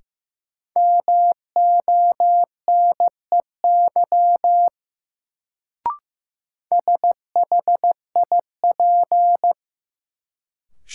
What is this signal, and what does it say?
Morse code, heard as a keyed tone, about 15 wpm, 700 Hz. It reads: MONEY SHIP
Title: Sentences from Top 400 Words - New Word Drill - 15wpm